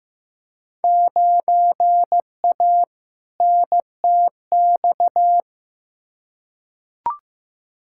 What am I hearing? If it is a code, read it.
Morse code, heard as a keyed tone, about 15 wpm, 700 Hz. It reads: 9A NTX
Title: Field Day Exchanges Encore - 15wpm